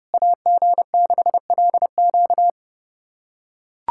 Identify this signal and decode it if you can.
Morse code, heard as a keyed tone, about 30 wpm, 700 Hz. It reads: AG6LQ
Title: USA Callsign DE Callsign - 5 Characters - 30wpm